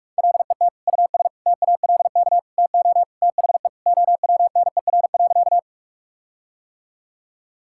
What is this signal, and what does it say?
Morse code, heard as a keyed tone, about 46 wpm, 700 Hz. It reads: LET US TALK TO THE OWNER.
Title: Sentences from Top 1600 Words 45wpm